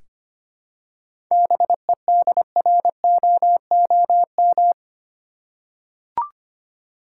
Morse code, heard as a keyed tone, about 25 wpm, 700 Hz. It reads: BEDROOM
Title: Seven Letter Words - 25wpm